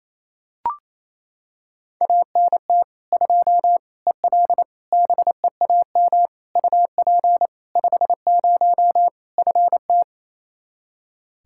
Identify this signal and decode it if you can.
Morse code, heard as a keyed tone, about 28 wpm, 700 Hz. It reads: ANT 2 EL BEAM UP 50 FT
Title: K1USN SST QSO Practice 28wpm